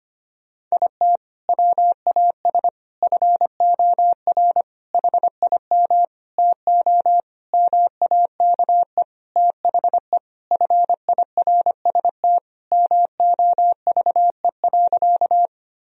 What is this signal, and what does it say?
Morse code, heard as a keyed tone, about 25 wpm, 700 Hz. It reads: IT WAS FOR HIM TO MAKE THE FIRST MOVE.
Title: Sentences from Top 1000 Words - Review Part 1 - 25wpm